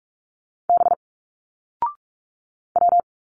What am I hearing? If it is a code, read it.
Morse code, heard as a keyed tone, about 42 wpm, 700 Hz. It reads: B W
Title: Single Letters 45wpm